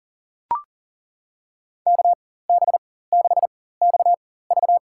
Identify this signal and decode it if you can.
Morse code, heard as a keyed tone, about 37 wpm, spaced as for 18 wpm, 700 Hz. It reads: KB6XV
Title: USA Call Signs - 5 Characters - 4x Character Spacing - 40wpm